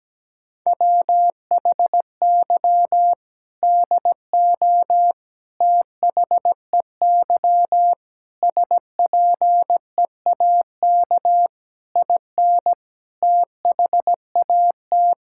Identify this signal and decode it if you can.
Morse code, heard as a keyed tone, about 17 wpm, 700 Hz. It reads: WHY DO THEY SPEAK IN THAT
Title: Sentences from Top 1600 Words 17wpm